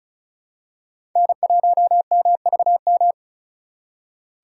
Morse code, heard as a keyed tone, about 36 wpm, 700 Hz. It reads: N1MVM
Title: USA Call Signs - 5 Characters - 3x Character Spacing - 35wpm